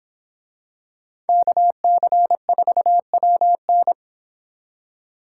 Morse code, heard as a keyed tone, about 26 wpm, 700 Hz. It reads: KC4WN